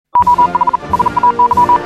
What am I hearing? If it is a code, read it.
Morse code, heard as a keyed tone, about 40 wpm, 980 Hz. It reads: OH SAY
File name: morse hymne(1)